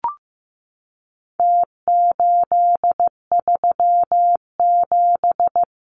Morse code, heard as a keyed tone, about 15 wpm, 700 Hz. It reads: T837
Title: Numbers - Contest Style - 4-Digit 15wpm